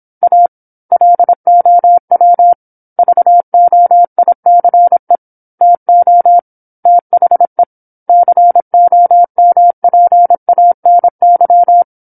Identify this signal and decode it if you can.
Morse code, heard as a keyed tone, about 27 wpm, 700 Hz. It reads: A LOW VOICE TO THE COMPANY
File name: 08-Chapter-08 26 WPM